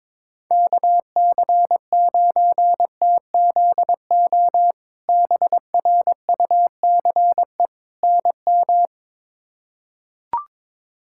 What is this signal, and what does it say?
Morse code, heard as a keyed tone, about 21 wpm, 700 Hz. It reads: KC9TZO BRUCE NM
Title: K1USN SST QSO Practice - 22wpm